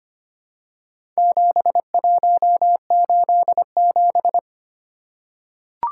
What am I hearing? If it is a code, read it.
Morse code, heard as a keyed tone, about 25 wpm, 700 Hz. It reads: 7187
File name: practice / 25wpm / Four Letter-Number Combinations - 25wpm